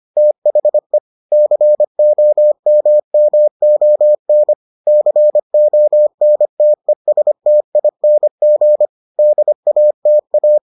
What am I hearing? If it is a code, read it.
Morse code, heard as a keyed tone, about 25 wpm, 600 Hz. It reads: THE COMMON CONTESTING DATA